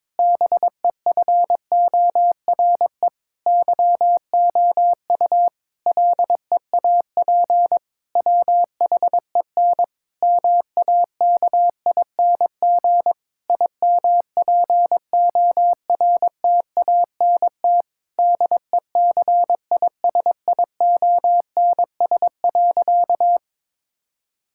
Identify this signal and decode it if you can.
Morse code, heard as a keyed tone, about 22 wpm, 700 Hz. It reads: BEFORE YOU LEAP WHEN MAKING IMPORTANT DECISIONS.
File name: Difficult Sentences with Hints - 22wpm